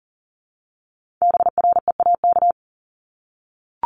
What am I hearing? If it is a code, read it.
Morse code, heard as a keyed tone, about 40 wpm, 700 Hz. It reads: BREAK
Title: Multi-letter Phonemes Repeated 3x 40wpm